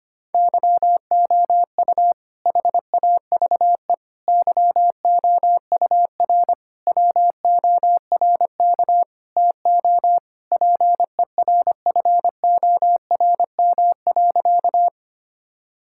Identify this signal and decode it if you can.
Morse code, heard as a keyed tone, about 25 wpm, 700 Hz. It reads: YOU HAVE YOUR WORK TO PERFORM.
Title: Sentences from Top 1000 Words - Review Part 5 - 25wpm